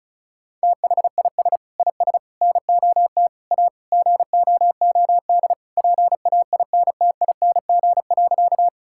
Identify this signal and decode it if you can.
Morse code, heard as a keyed tone, about 34 wpm, 700 Hz. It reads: THIS IS NOT A GOOD PAINTING.
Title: Sentences from Top 1000 Words - Review Part 6 - 35wpm